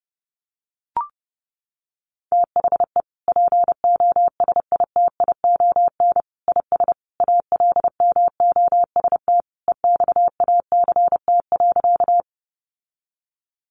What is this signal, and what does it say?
Morse code, heard as a keyed tone, about 30 wpm, 700 Hz. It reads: THE POSITION IS ALMOST EXACT.
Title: Sentences from Top 1300 Words 30wpm